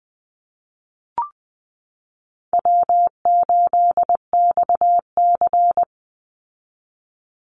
Morse code, heard as a keyed tone, about 20 wpm, 700 Hz. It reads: W8XC
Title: Easy Call Signs Repeated 2x - 20wpm